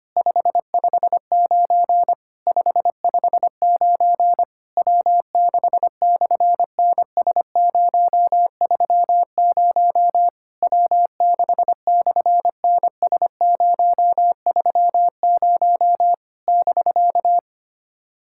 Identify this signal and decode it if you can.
Morse code, heard as a keyed tone, about 26 wpm, 700 Hz. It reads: 559 559 W6/NS030 W6/NS030 <BK>
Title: SOTA QSO Practice 25wpm